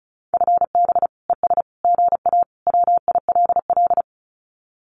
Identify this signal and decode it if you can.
Morse code, heard as a keyed tone, about 34 wpm, 700 Hz. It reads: FB ES GA WILL
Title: Common QSO Phrases Encore 35wpm